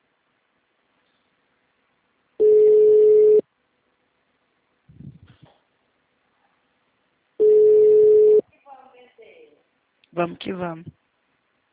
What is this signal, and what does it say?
Ringback tone (Europe, 425 Hz), with speech.